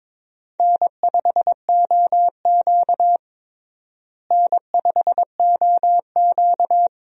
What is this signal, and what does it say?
Morse code, heard as a keyed tone, about 22 wpm, 700 Hz. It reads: N5OQ N5OQ
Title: USA Call Signs - 4 Characters - Repeated 2x - 22wpm